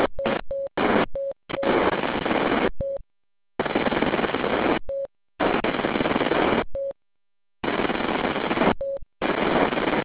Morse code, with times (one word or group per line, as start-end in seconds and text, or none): none